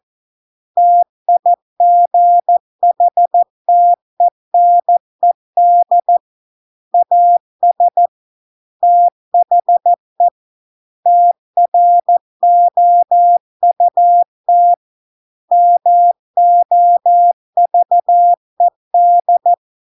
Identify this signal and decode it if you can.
Morse code, heard as a keyed tone, about 14 wpm, 700 Hz. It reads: TIGHTENED AS THE TROUT MOVED